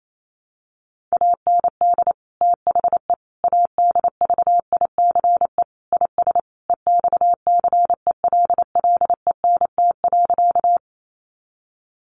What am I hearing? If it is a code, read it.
Morse code, heard as a keyed tone, about 28 wpm, 700 Hz. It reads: AND THE ADVICE IS EXCELLENT.
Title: Sentences from Top 1100 Words 28wpm